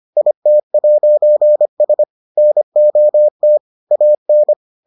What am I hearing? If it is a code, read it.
Morse code, heard as a keyed tone, about 25 wpm, 600 Hz. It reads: IT'S NOT AN